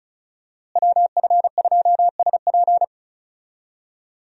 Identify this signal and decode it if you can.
Morse code, heard as a keyed tone, about 36 wpm, 700 Hz. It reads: WF2SP
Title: USA Call Signs - 5 Characters - Repeated 3x - 35wpm